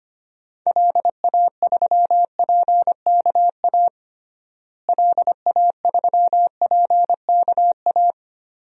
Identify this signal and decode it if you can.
Morse code, heard as a keyed tone, about 25 wpm, 700 Hz. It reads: LA3PKA LA3PKA
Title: DXCC-100 Call Signs - 6 Characters - Repeated 2x - 25wpm